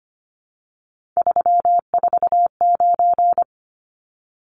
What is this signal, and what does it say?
Morse code, heard as a keyed tone, about 25 wpm, 700 Hz. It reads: 349
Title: Numbers - 3-Digit 25wpm